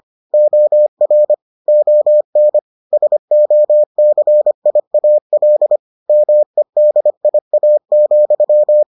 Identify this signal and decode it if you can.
Morse code, heard as a keyed tone, about 25 wpm, 600 Hz. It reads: OR ON SOCIAL MEDIA,